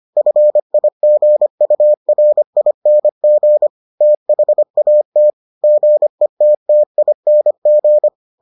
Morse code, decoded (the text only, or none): FIGURING THAT GETTING